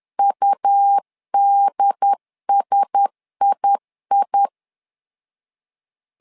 Morse code, heard as a keyed tone, about 10 wpm, 800 Hz. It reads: UDSII
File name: LEZIONI / 5 LBVC / 5 L 10wpm 2259